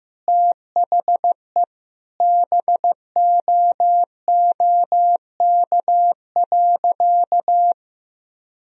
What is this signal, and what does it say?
Morse code, heard as a keyed tone, about 15 wpm, 700 Hz. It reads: THE BOOK.
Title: Sentences from Top 1000 Words - Review Part 1 - 15wpm